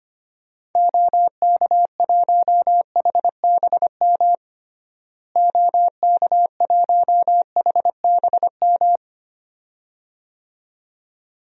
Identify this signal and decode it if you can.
Morse code, heard as a keyed tone, about 25 wpm, 700 Hz. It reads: OK1HBM OK1HBM
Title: DXCC-100 Call Signs - 6 Characters - Repeated 2x - 25wpm